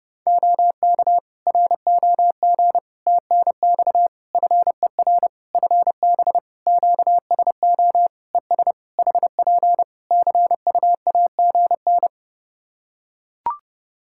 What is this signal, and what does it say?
Morse code, heard as a keyed tone, about 30 wpm, 700 Hz. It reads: OK ROG TNX FER FB QSO ES HP CUAGN
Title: Practice QSO - Exchange 4 - Ending - 30wpm